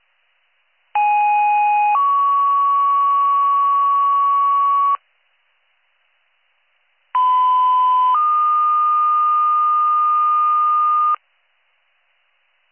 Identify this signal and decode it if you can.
Two-tone page, heard as a one-second tone, then a three-second tone, about 830 then 1120 Hz, about 1010 then 1220 Hz. It is heard from 0.9 to 11.1 s.